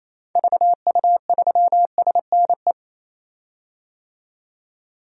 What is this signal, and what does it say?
Morse code, heard as a keyed tone, about 28 wpm, 700 Hz. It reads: VU3SNE
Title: DXCC-100 Call Signs - 6 Characters - 28wpm